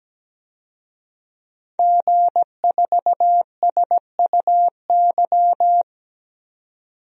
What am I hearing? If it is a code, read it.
Morse code, heard as a keyed tone, about 17 wpm, 700 Hz. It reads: G4SUY
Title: DXCC-100 Call Signs - 5 Characters - 17wpm